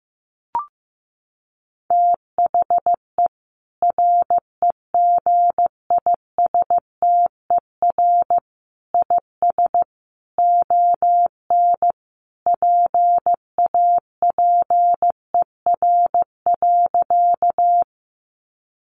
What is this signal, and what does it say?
Morse code, heard as a keyed tone, about 15 wpm, 700 Hz. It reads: THE REGISTER IS ON PAPER.
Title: Sentences from Top 1200 Words 15wpm